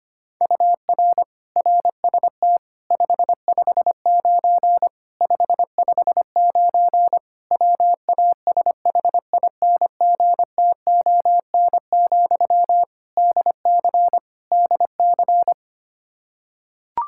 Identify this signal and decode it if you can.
Morse code, heard as a keyed tone, about 25 wpm, 700 Hz. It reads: UR RST 559 559 WASHINGTON, DC DC
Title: Practice QSO - Exchange 1 - Beginning - 25wpm